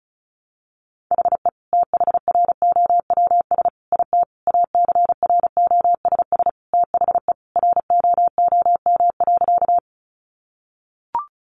Morse code, heard as a keyed tone, about 34 wpm, 700 Hz. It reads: HE THROWS IT ACROSS THE ROOM.
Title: Additional Easy Phrases 35wpm